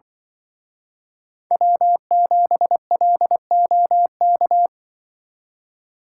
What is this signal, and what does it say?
Morse code, heard as a keyed tone, about 25 wpm, 700 Hz. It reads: W7LOK